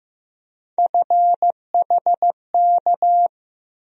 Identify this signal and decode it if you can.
Morse code, heard as a keyed tone, about 15 wpm, 700 Hz. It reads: FHK
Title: USA Call Sign Postfixes - 3 Letters 15wpm